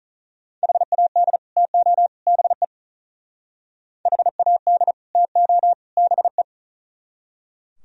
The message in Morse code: HAD TO BE HAD TO BE

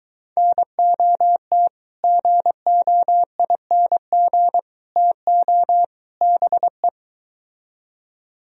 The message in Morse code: NOT GOING TO BE

